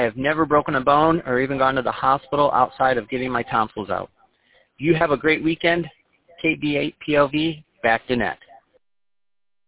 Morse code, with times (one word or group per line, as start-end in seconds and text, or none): none